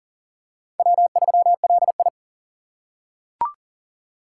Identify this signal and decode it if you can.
Morse code, heard as a keyed tone, about 40 wpm, 700 Hz. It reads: W3LI
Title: USA Call Signs - 3x Character Spacing - 40wpm